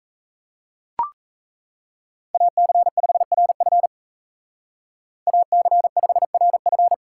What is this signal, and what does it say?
Morse code, heard as a keyed tone, about 33 wpm, 700 Hz. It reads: AC5RF AC5RF
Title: Speed Racing - USA 2x2 Call Signs 28wpm